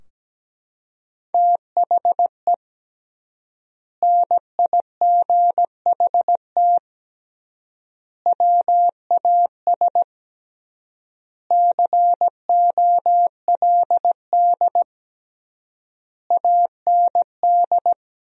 Morse code, heard as a keyed tone, about 17 wpm, 700 Hz. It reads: THE NIGHT WAS COLD AND
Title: 3x Word-Spacing - Sentences from Top 400 Words - 17wpm